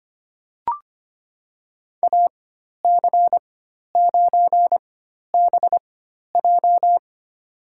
Morse code, heard as a keyed tone, about 25 wpm, spaced as for 12 wpm, 700 Hz. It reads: AC9BJ
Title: USA Call Signs - 5 Characters - 4x Character Spacing - 25wpm